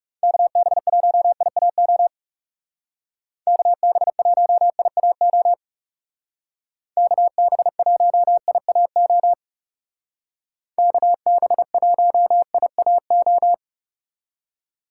Morse code, heard as a keyed tone, about 37 wpm, 700 Hz. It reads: KB1IAO KB1IAO KB1IAO KB1IAO
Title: Speed Racing - USA 2x3 Call Signs 30wpm